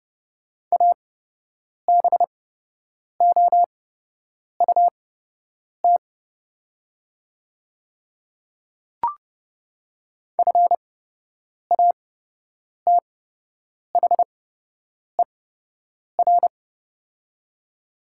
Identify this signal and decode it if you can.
Morse code, heard as a keyed tone, about 30 wpm, spaced as for 8 wpm, 700 Hz. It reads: ABOUT FATHER
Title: 8x Character Spacing - Top 200 Words - 30wpm